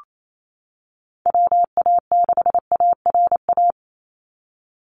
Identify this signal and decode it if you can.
Morse code, heard as a keyed tone, about 28 wpm, 700 Hz. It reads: WA6ARA
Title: NA SOTA Chaser Callsigns 28wpm